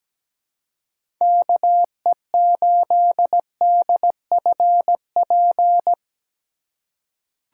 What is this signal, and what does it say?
Morse code, heard as a keyed tone, about 17 wpm, 700 Hz. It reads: KE8DFP
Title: USA Call Signs - 6 Characters - 3x Character Spacing - 17wpm